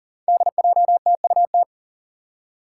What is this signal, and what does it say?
Morse code, heard as a keyed tone, about 40 wpm, 700 Hz. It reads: DJTUT
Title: Five Challenging Letter-Number Combinations 40wpm